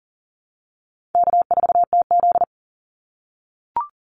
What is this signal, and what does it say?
Morse code, heard as a keyed tone, about 40 wpm, 700 Hz. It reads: K4TZ
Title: Call Signs - 40wpm